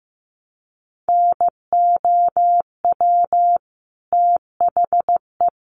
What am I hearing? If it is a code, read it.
Morse code, heard as a keyed tone, about 15 wpm, 700 Hz. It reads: NOW THE
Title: Sentences from Top 1200 Words 15wpm